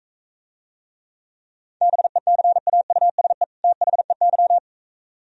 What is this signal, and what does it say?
Morse code, heard as a keyed tone, about 44 wpm, 690 Hz. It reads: BECAUSE THEY